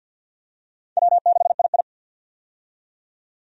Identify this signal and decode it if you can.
Morse code, heard as a keyed tone, about 47 wpm, 700 Hz. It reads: W6II
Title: USA Call Signs - 4 Characters - 50wpm